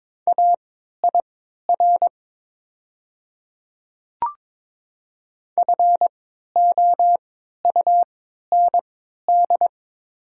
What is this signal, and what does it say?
Morse code, heard as a keyed tone, about 23 wpm, spaced as for 13 wpm, 700 Hz. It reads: AIR FOUND
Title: 3x Character Spacing - Top 300 Words - 22wpm